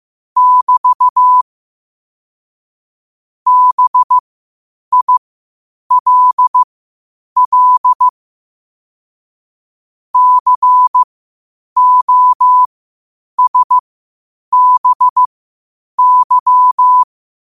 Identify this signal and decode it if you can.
Morse code, heard as a keyed tone, about 15 wpm, spaced as for 8 wpm, 1000 Hz. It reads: <BT> BILL COSBY
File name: qotdfiles / 8wpm